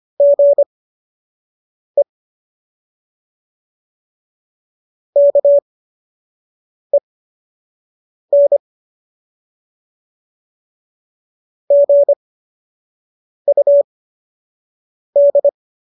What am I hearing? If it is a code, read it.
Morse code, heard as a keyed tone, about 25 wpm, spaced as for 6 wpm, 570 Hz. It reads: GE KEN GUD
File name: qso003 06